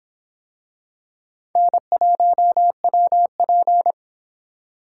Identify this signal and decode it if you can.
Morse code, heard as a keyed tone, about 26 wpm, 700 Hz. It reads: N1WP